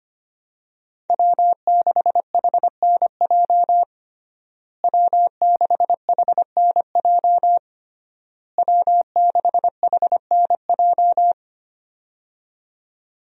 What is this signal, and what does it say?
Morse code, heard as a keyed tone, about 25 wpm, 700 Hz. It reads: W6HNJ W6HNJ W6HNJ
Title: USA Call Signs - 5 Characters - Repeated 3x - 25wpm